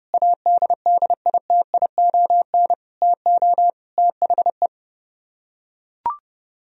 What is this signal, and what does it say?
Morse code, heard as a keyed tone, about 31 wpm, 700 Hz. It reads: ADDITION TO THE